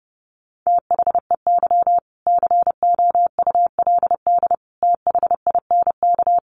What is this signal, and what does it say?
Morse code, heard as a keyed tone, about 30 wpm, 700 Hz. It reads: THEY COULD THINK
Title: Sentences from Top 800 Words 30wpm